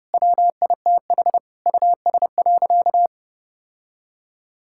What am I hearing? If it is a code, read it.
Morse code, heard as a keyed tone, about 30 wpm, 700 Hz. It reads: WITH US.
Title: Sentences from Top 1600 Words 30wpm